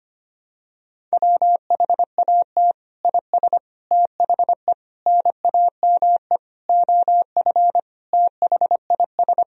Morse code, heard as a keyed tone, about 25 wpm, 700 Hz. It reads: WHAT IS THE NAME OF THIS
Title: Sentences from Top 1000 Words - Review Part 1 - 25wpm